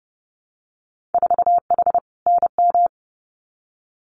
Morse code, heard as a keyed tone, about 30 wpm, 700 Hz. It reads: 4H NM
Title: Winter Field Day Exchanges Encore 30wpm